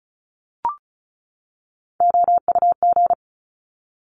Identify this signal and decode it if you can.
Morse code, heard as a keyed tone, about 36 wpm, 690 Hz. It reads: OUG
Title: Top Three Letter Combinations in Words 35wpm